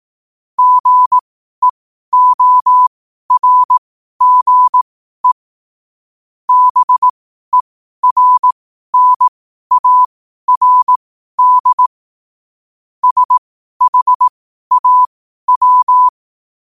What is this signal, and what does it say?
Morse code, heard as a keyed tone, about 18 wpm, spaced as for 12 wpm, 1000 Hz. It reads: GEORGE BERNARD SHAW
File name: qotdfiles / 12wpm